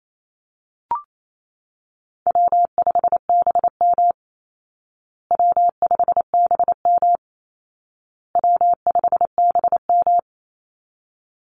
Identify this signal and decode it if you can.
Morse code, heard as a keyed tone, about 28 wpm, 700 Hz. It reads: W5BM W5BM W5BM
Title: Easy Call Signs Repeated 3x - 28wpm